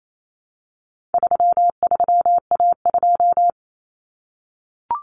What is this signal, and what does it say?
Morse code, heard as a keyed tone, about 28 wpm, 700 Hz. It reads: 33A2
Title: Common Cut Numbers - 4-Digit 28wpm